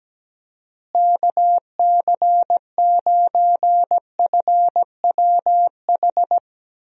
Morse code, heard as a keyed tone, about 17 wpm, 700 Hz. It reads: KC9FWH
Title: USA Call Signs - 6 Characters - Repeated 2x - 17wpm